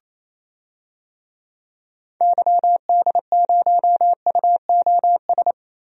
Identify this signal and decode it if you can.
Morse code, heard as a keyed tone, about 28 wpm, 700 Hz. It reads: YD0UOS
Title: DXCC-100 Call Signs - 6 Characters - Repeated 2x - 28wpm